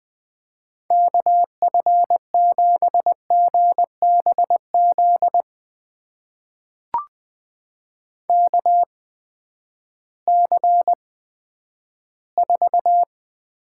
Morse code, heard as a keyed tone, about 21 wpm, spaced as for 11 wpm, 700 Hz. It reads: KF7GBZ KC4
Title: USA Call Signs - 6 Characters - 8x Character Spacing - 20wpm